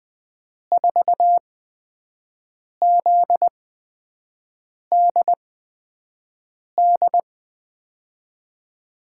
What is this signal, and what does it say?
Morse code, heard as a keyed tone, about 20 wpm, spaced as for 5 wpm, 700 Hz. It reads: 4ZDD
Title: USA Call Signs - 8x Character Spacing - 20wpm